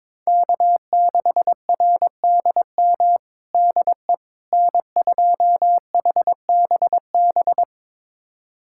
Morse code, heard as a keyed tone, about 22 wpm, 700 Hz. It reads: K6RDM DE N2HBB
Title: USA Callsign DE Callsign - 5 Characters - 22wpm